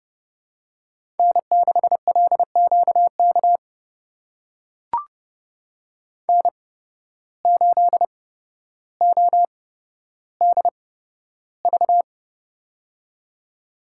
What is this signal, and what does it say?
Morse code, heard as a keyed tone, about 31 wpm, spaced as for 13 wpm, 700 Hz. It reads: N6LQK N8ODV
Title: USA Call Signs - 8x Character Spacing - 30wpm